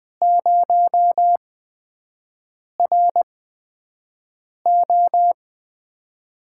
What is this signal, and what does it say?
Morse code, heard as a keyed tone, about 20 wpm, spaced as for 5 wpm, 700 Hz. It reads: 0RO
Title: USA Call Signs - 5 Characters - 8x Character Spacing - 20wpm